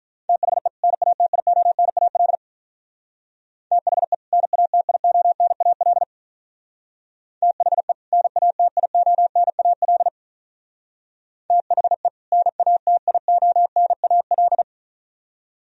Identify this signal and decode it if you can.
Morse code, heard as a keyed tone, about 44 wpm, 700 Hz. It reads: THE NATIONAL THE NATIONAL THE NATIONAL THE NATIONAL